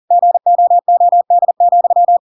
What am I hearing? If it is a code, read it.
Morse code, heard as a keyed tone, about 40 wpm, 700 Hz. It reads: GOOD,
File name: Pinocchio / Pinocchio ch4 40 40 ik1hge